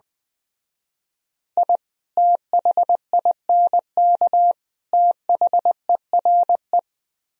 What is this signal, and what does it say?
Morse code, heard as a keyed tone, about 19 wpm, 700 Hz. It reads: I THINK THERE
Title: Speed-Racing - Sets of 3 Words 17wpm